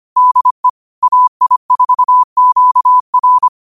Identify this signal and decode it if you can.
Morse code, heard as a keyed tone, about 25 wpm, 1000 Hz. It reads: DE AI4QR
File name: qotdfiles / 25wpm